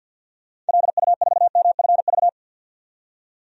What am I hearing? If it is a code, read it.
Morse code, heard as a keyed tone, about 53 wpm, 700 Hz. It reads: LU4MFV
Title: DXCC-100 Call Signs - 6 Characters - 50wpm